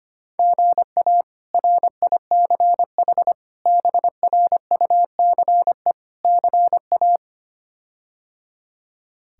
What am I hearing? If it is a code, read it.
Morse code, heard as a keyed tone, about 25 wpm, 700 Hz. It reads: GA RICH BRUCE CA
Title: K1USN SST QSO Practice - 25wpm